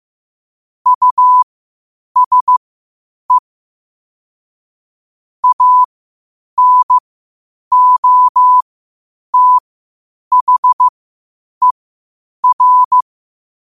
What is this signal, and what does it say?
Morse code, heard as a keyed tone, about 15 wpm, spaced as for 8 wpm, 1000 Hz. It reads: USE ANOTHER
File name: qotdfiles / 8wpm